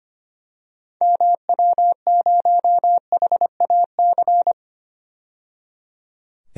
Morse code, heard as a keyed tone, about 25 wpm, 700 Hz. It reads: MW0HAC
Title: DXCC-100 Call Signs - 6 Characters - 25wpm